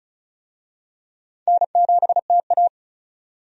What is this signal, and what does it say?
Morse code, heard as a keyed tone, about 36 wpm, 700 Hz. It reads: N7TA
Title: USA Call Signs - 4 Characters - 35wpm